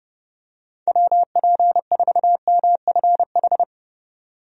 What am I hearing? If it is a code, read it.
Morse code, heard as a keyed tone, about 31 wpm, 700 Hz. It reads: WP4MFH
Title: USA Call Signs - 6 Characters - 30wpm